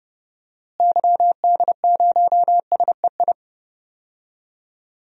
Morse code, heard as a keyed tone, about 31 wpm, 700 Hz. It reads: YD0SEI
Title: DXCC-100 Call Signs - 6 Characters - Repeated 2x - 30wpm